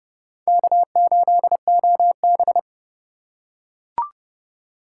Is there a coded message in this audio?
Yes: Morse code from 0.5 to 2.6 s, heard as a keyed tone, about 26 wpm, 700 Hz. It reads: K8OB